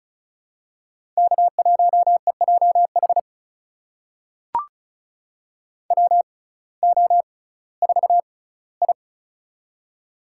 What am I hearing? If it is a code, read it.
Morse code, heard as a keyed tone, about 36 wpm, spaced as for 19 wpm, 700 Hz. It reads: K1EJH WO4I
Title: USA Call Signs - 6x Character Spacing - 35wpm